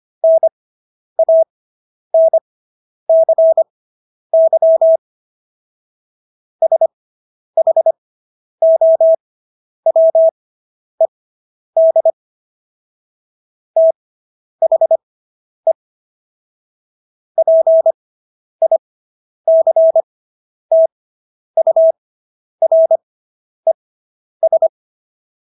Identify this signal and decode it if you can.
Morse code, heard as a keyed tone, about 25 wpm, spaced as for 10 wpm, 650 Hz. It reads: NANCY SHOWED THE PICTURES